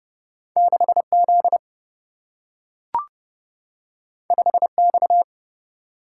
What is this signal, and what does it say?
Morse code, heard as a keyed tone, about 30 wpm, 700 Hz. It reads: 6Z 5X
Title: Single-Letter Single-Number 30wpm